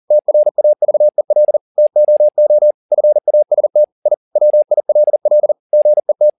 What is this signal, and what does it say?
Morse code, heard as a keyed tone, about 40 wpm, 600 Hz. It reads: TRAVEL TOO FAST I WILL GET